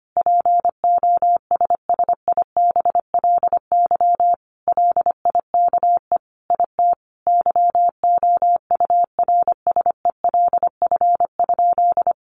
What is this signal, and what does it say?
Morse code, heard as a keyed tone, about 25 wpm, 700 Hz. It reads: POSSIBLY LIKE IT YOURSELF?
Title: Sentences from Top 1300 Words 25wpm